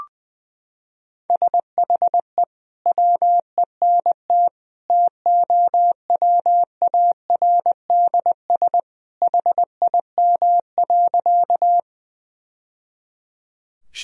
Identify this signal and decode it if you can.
Morse code, heard as a keyed tone, about 21 wpm, 700 Hz. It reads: SHE WENT TOWARDS HIM.
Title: Sentences from Top 1000 Words - Review Part 2 - 20wpm